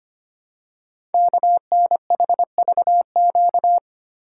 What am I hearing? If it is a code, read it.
Morse code, heard as a keyed tone, about 25 wpm, 700 Hz. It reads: KNHVQ
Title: Code Groups - 5 Letters - 25wpm